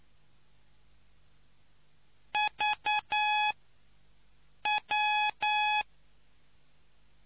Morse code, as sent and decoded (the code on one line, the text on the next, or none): ...- .--
VW